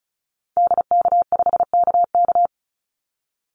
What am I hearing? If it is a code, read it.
Morse code, heard as a keyed tone, about 34 wpm, 700 Hz. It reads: DK5KK
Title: Call Signs Repeated 2x - 35wpm